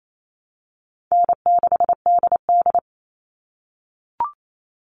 Morse code, heard as a keyed tone, about 28 wpm, 700 Hz. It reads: N6DD
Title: All Cut Numbers - 4-Digit 28wpm